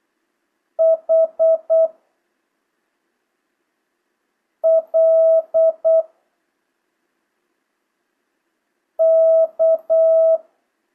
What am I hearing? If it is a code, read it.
Morse code, heard as a keyed tone, about 8 wpm, 650 Hz. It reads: H L K
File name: jh8cbh / 2025cw